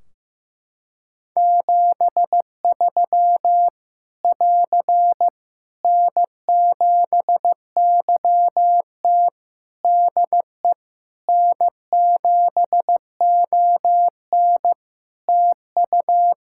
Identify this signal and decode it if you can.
Morse code, heard as a keyed tone, about 15 wpm, 700 Hz. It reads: 73 <AR> N7YT DE N7ON TU
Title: Practice QSO - Exchange 4 - Ending - 15wpm